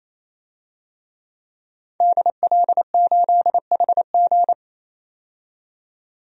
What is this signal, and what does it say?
Morse code, heard as a keyed tone, about 28 wpm, 700 Hz. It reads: DL8HG
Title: DXCC-100 Call Signs - 5 Characters - 28wpm